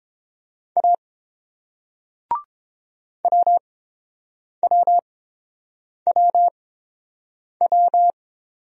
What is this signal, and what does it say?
Morse code, heard as a keyed tone, about 30 wpm, spaced as for 7 wpm, 700 Hz. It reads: A WWWW